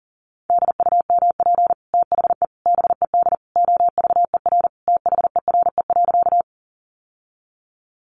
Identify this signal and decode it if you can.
Morse code, heard as a keyed tone, about 40 wpm, 700 Hz. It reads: DUMP THE BED OVER THERE.
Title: Sentences from Top 1300 Words 40wpm